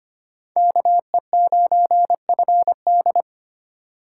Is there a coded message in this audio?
Yes: Morse code from 0.6 to 3.2 s, heard as a keyed tone, about 32 wpm, 700 Hz. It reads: KE9FD